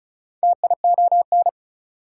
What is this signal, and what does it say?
Morse code, heard as a keyed tone, about 34 wpm, 700 Hz. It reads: TION
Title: Prefixes and Suffixes 35wpm